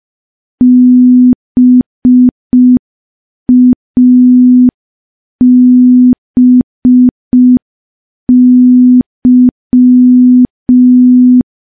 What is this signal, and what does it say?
Morse code, heard as a keyed tone, about 5 wpm, 250 Hz. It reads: BABY
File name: morse / morse-training-05WPM